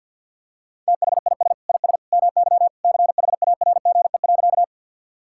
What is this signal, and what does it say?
Morse code, heard as a keyed tone, about 50 wpm, 700 Hz. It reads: THIS IS MY CHARGE.
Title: Sentences from Top 1000 Words - Review Part 6 - 50wpm